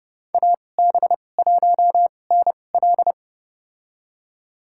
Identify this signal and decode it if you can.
Morse code, heard as a keyed tone, about 30 wpm, spaced as for 22 wpm, 700 Hz. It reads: AB1NL
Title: USA Call Signs - 2x Character Spacing - 30wpm